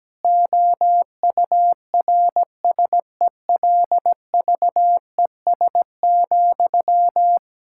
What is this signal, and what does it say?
Morse code, heard as a keyed tone, about 17 wpm, 700 Hz. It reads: OURSELVES,
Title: Sentences from Top 1600 Words 17wpm